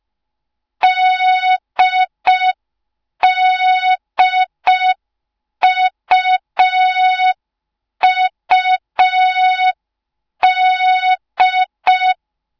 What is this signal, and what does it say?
Morse code, heard as a keyed tone, about 5 wpm, 3700 Hz. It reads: DDUUD